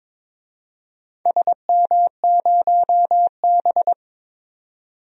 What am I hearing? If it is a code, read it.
Morse code, heard as a keyed tone, about 22 wpm, 700 Hz. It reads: SM0B
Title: DXCC-100 Call Signs - 4 Characters - 22wpm